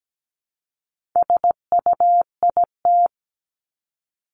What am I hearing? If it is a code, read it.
Morse code, heard as a keyed tone, about 17 wpm, 700 Hz. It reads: SUIT
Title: Top 1501-1750 Words 17wpm